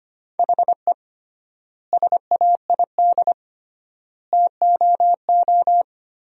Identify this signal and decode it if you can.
Morse code, heard as a keyed tone, about 25 wpm, 700 Hz. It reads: HE SAID TOO
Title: 3x Word-Spacing - Sentences from Top 200 Words - 25wpm